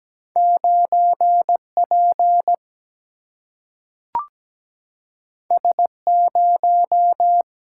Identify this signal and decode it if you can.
Morse code, heard as a keyed tone, about 17 wpm, 700 Hz. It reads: 9P S0
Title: Single-Letter Single-Number 17wpm